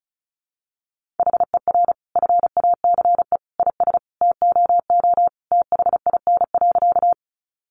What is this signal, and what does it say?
Morse code, heard as a keyed tone, about 34 wpm, 700 Hz. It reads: HER FACE IS TOO THIN.
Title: Sentences from Top 1200 Words 35wpm